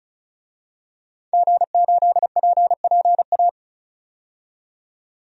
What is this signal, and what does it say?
Morse code, heard as a keyed tone, about 34 wpm, 700 Hz. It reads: G8PPA
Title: DXCC-100 Call Signs - 5 Characters - 35wpm